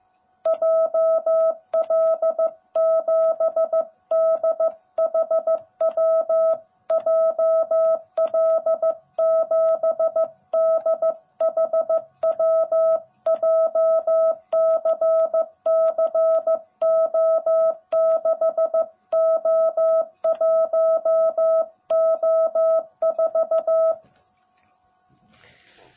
Morse code, heard as a keyed tone, about 15 wpm, 660 Hz. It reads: JL7DHW JL7DHW JCCO6O1O4
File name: jh8cbh / 2021cw / cw18b